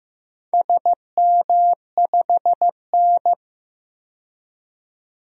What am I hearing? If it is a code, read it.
Morse code, heard as a keyed tone, about 15 wpm, 700 Hz. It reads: SM5N
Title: DXCC-100 Call Signs - 4 Characters - 15wpm